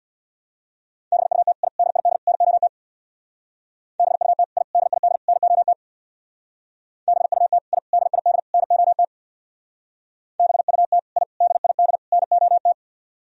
Morse code, heard as a keyed tone, about 62 wpm, 700 Hz. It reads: BUT I DID NOT BUT I DID NOT BUT I DID NOT BUT I DID NOT
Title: Speed-Racing - Sets of 4 Words 50wpm